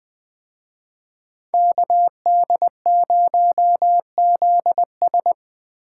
Morse code, heard as a keyed tone, about 21 wpm, 700 Hz. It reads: KD0ZS
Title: Speed Racing - USA 2x2 Call Signs 17wpm